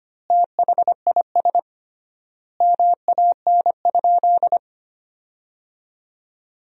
Morse code, heard as a keyed tone, about 25 wpm, 700 Hz. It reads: THIS MAN?
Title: 3x Word-Spacing - Sentences from Top 200 Words - 25wpm